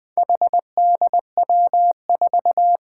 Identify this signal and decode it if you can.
Morse code, heard as a keyed tone, about 20 wpm, 700 Hz. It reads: HDW4
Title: Four Challenging Letter-Number Combinations 20wpm